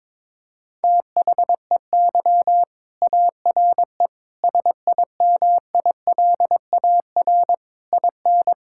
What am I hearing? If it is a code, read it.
Morse code, heard as a keyed tone, about 22 wpm, 700 Hz. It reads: THEY ARE SIMILAR IN
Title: Sentences from Top 1000 Words - Review Part 5 - 22wpm